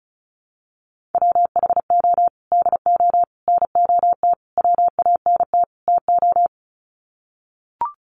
Morse code, heard as a keyed tone, about 34 wpm, 700 Hz. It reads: WHO DO NOT WANT TO
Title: Sets of 5 Words Encore2 35wpm